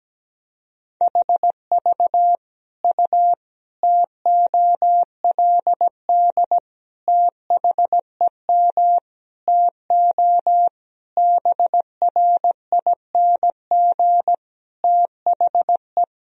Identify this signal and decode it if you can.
Morse code, heard as a keyed tone, about 17 wpm, 700 Hz. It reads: HV U TOLD THEM TO BRING THE
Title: Sentences from Top 500 Words and Abbreviations 17wpm